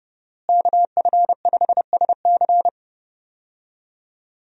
Morse code, heard as a keyed tone, about 30 wpm, 700 Hz. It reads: KF5SC
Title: USA Call Signs - 5 Characters - Repeated 2x - 30wpm